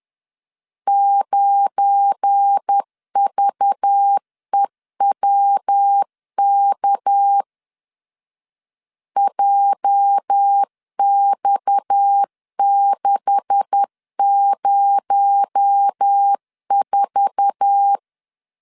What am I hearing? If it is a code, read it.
Morse code, heard as a keyed tone, about 10 wpm, 800 Hz. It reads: 9VEWK JX604
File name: CW / DET / 1 9 1500C S B 10wpm 4439